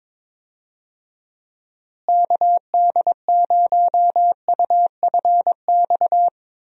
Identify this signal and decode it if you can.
Morse code, heard as a keyed tone, about 21 wpm, 700 Hz. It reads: KD0UFX
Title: USA Call Signs - 6 Characters - Repeated 2x - 22wpm